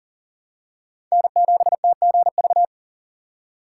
Morse code, heard as a keyed tone, about 40 wpm, 700 Hz. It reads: N7TGV
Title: USA Call Signs - 4x Character Spacing - 40wpm